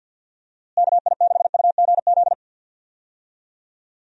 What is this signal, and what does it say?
Morse code, heard as a keyed tone, about 52 wpm, 700 Hz. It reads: KI6UGZ
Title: USA Call Signs - 6 Characters - Repeated 2x - 50wpm